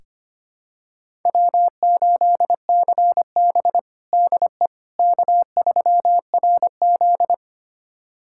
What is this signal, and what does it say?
Morse code, heard as a keyed tone, about 25 wpm, 700 Hz. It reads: W8CB DE K3RZ
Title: USA Callsign DE Callsign - 4 Characters - 25wpm